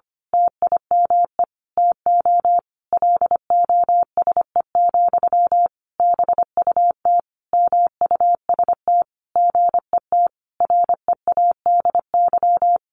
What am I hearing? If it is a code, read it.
Morse code, heard as a keyed tone, about 25 wpm, 700 Hz. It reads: TIME TO LOSE, BUT MUST GET READY
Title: Sentences from Top 1100 Words 25wpm